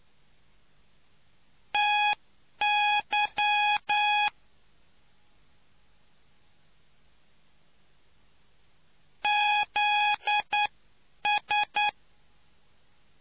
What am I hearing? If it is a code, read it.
Morse code, heard as a keyed tone, about 9 wpm, spaced as for 8 wpm, 830 Hz. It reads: TY ZS